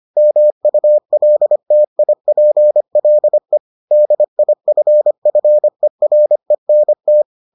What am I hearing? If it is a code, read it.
Morse code, heard as a keyed tone, about 25 wpm, 600 Hz. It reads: MULTIPLE DIFFERENT